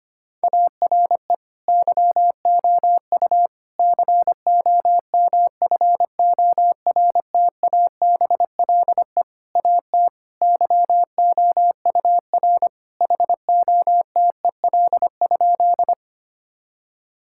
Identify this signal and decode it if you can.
Morse code, heard as a keyed tone, about 25 wpm, 700 Hz. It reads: ARE YOU COMFORTABLE AT YOUR HOTEL?
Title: Sentences from Top 1600 Words 25wpm